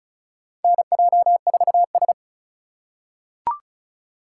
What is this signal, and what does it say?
Morse code, heard as a keyed tone, about 36 wpm, 700 Hz. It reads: NJ4S
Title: USA Call Signs - 4 Characters - 4x Character Spacing - 35wpm